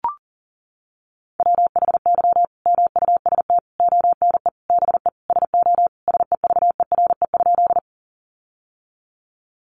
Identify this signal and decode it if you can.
Morse code, heard as a keyed tone, about 40 wpm, 700 Hz. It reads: WHY MUST ONE BE SO SEVERE?
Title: Sentences from Top 1400 Words 40wpm